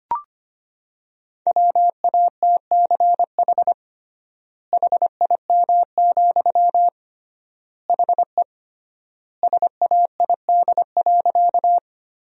Morse code, heard as a keyed tone, about 25 wpm, 700 Hz. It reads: WATCH HIM, HE SAID.
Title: 3x Word-Spacing - Sentences from Top 300 Words - 25wpm